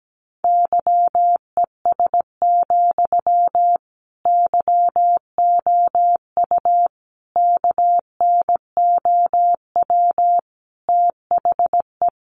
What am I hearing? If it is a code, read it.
Morse code, heard as a keyed tone, about 17 wpm, 700 Hz. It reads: YES, YOU KNOW THE
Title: Sentences from Top 700 Words 17wpm